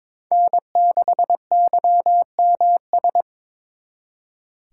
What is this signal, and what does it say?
Morse code, heard as a keyed tone, about 26 wpm, 700 Hz. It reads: N6YMS
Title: Speed Racing - USA 1x3 Call Signs 22wpm